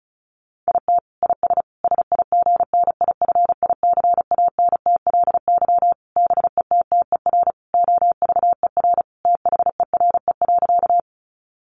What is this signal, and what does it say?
Morse code, heard as a keyed tone, about 34 wpm, 700 Hz. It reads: IT IS SIGNIFICANTLY BETTER OVER THERE.
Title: Sentences from Top 1500 Words 35wpm